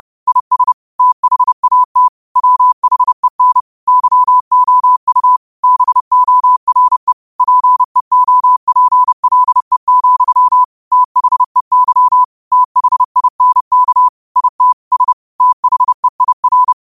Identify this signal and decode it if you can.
Morse code, heard as a keyed tone, about 30 wpm, 1000 Hz. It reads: IS THAT WHEN YOU BORE PEOPLE, THEY THINK IT S THEIR
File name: qotdfiles / 30wpm